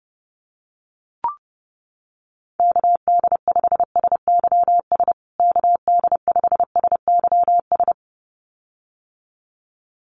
Morse code, heard as a keyed tone, about 30 wpm, 700 Hz. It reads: KD5SYS KD5SYS
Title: USA Call Signs Repeated 2x 30wpm